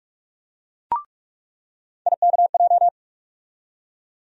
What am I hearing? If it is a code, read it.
Morse code, heard as a keyed tone, about 46 wpm, 700 Hz. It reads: IKJ
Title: USA Call Sign Postfixes - 3 Letters 45wpm